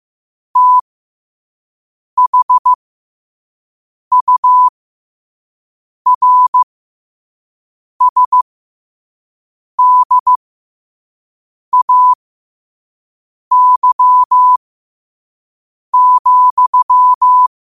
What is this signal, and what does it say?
Morse code, heard as a keyed tone, about 15 wpm, spaced as for 5 wpm, 1000 Hz. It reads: THURSDAY,
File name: qotdfiles / 5wpm